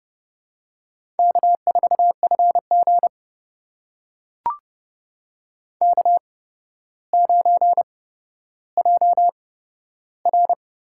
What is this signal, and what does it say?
Morse code, heard as a keyed tone, about 30 wpm, spaced as for 13 wpm, 700 Hz. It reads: K4FG K9JR
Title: USA Call Signs - 4 Characters - 8x Character Spacing - 30wpm